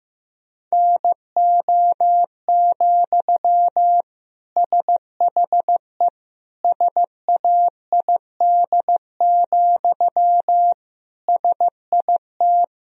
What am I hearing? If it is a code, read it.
Morse code, heard as a keyed tone, about 15 wpm, 700 Hz. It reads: NO, SHE SAID, SIT
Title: Sentences from Top 1000 Words - Review Part 1 - 15wpm